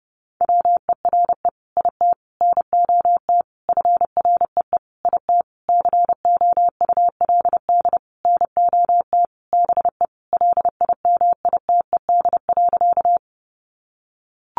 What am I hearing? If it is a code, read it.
Morse code, heard as a keyed tone, about 30 wpm, 700 Hz. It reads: WERE IT NOT FREE IT COULD NOT BE LIMITED.
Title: Sentences from Top 900 Words 30wpm